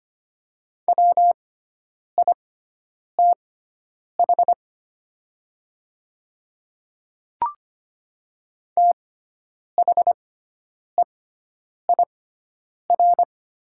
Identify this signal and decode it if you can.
Morse code, heard as a keyed tone, about 25 wpm, spaced as for 9 wpm, 690 Hz. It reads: WITH THEIR
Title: 6x Character Spacing - Top 200 Words - 25wpm